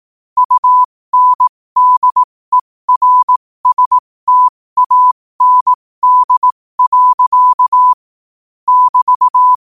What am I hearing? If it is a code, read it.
Morse code, heard as a keyed tone, about 18 wpm, spaced as for 15 wpm, 1000 Hz. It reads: UNDERSTAND. <BT>
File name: qotdfiles / 15wpm